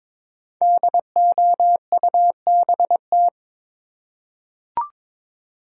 Morse code, heard as a keyed tone, about 22 wpm, 700 Hz. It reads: DOUBT
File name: 5 Letter Words - First 3 Letters Given - 22wpm